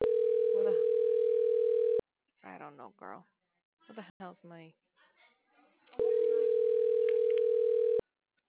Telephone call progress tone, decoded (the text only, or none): ringback tone